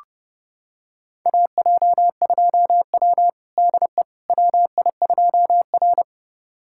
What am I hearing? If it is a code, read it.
Morse code, heard as a keyed tone, about 30 wpm, 700 Hz. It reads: AJ2W DE WI2R
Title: USA Callsign DE Callsign - 4 Characters - 30wpm